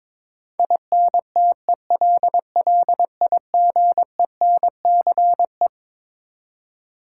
Morse code, heard as a keyed tone, about 22 wpm, 700 Hz. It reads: INTELLIGENCE
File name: Twelve Letter Words - 22wpm